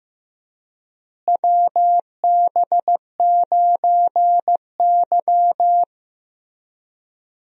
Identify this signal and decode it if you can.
Morse code, heard as a keyed tone, about 15 wpm, 700 Hz. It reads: WB9Y
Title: USA Call Signs - 4 Characters - Repeated 3x - 15wpm